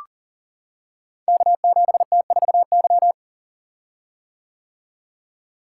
Morse code, heard as a keyed tone, about 42 wpm, 700 Hz. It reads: K7T4Y
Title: Code Groups - 5 Letter:Numbers - 40wpm